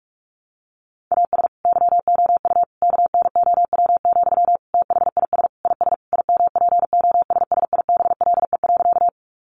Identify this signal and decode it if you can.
Morse code, heard as a keyed tone, about 44 wpm, 700 Hz. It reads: AS YOU KNOW, THIS IS IMPOSSIBLE.
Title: Sentences from Top 1100 Words 45wpm